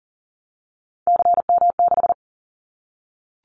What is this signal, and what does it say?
Morse code, heard as a keyed tone, about 40 wpm, 700 Hz. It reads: CM6
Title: Callsign Prefixes for Top 100 DXCC - 40wpm